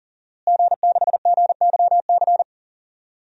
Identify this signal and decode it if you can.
Morse code, heard as a keyed tone, about 40 wpm, 700 Hz. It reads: G6GYC
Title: DXCC-100 Call Signs - 5 Characters - Repeated 2x - 40wpm